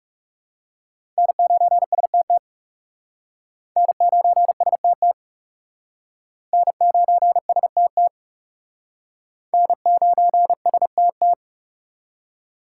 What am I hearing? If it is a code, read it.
Morse code, heard as a keyed tone, about 36 wpm, 700 Hz. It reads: N9STT N9STT N9STT N9STT
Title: Speed Racing - USA 1x3 Call Signs 30wpm